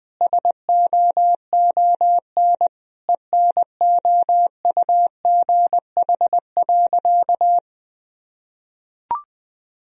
Morse code, heard as a keyed tone, about 21 wpm, 700 Hz. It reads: SOON ENOUGH.
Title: Sentences from Top 1000 Words - Review Part 4 - 20wpm